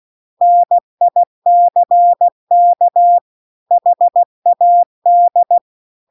Morse code, heard as a keyed tone, about 16 wpm, 700 Hz. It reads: NICK HAD